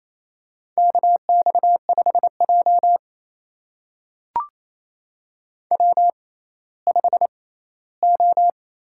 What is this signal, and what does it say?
Morse code, heard as a keyed tone, about 27 wpm, spaced as for 16 wpm, 700 Hz. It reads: KX5J W5O
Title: USA Call Signs - 4 Characters - 6x Character Spacing - 28wpm